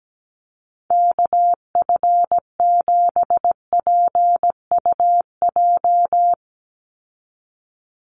Morse code, heard as a keyed tone, about 17 wpm, 700 Hz. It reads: KF7PUJ
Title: USA Call Signs 17wpm